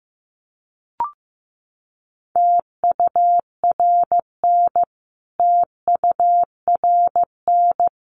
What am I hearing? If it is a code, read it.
Morse code, heard as a keyed tone, about 15 wpm, 700 Hz. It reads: TURN TURN
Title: Top 500 Words Repeated 15wpm